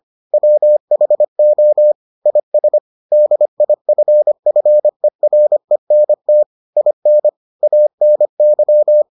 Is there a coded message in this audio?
Yes: Morse code from 0.3 to 9.0 s, heard as a keyed tone, about 25 wpm, 600 Hz. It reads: WHO IS DIFFERENT IN ANY